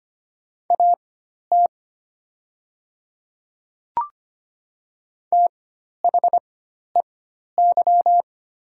Morse code, heard as a keyed tone, about 25 wpm, spaced as for 12 wpm, 700 Hz. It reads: AT THEY
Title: 4x Character Spacing - Top 100 Words - 25wpm